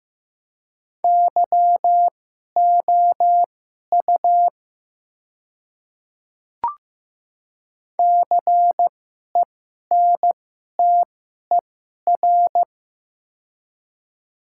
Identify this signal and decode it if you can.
Morse code, heard as a keyed tone, about 15 wpm, spaced as for 11 wpm, 700 Hz. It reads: YOU CENTER
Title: 2x Character Spacing - Top 500 Words - 15wpm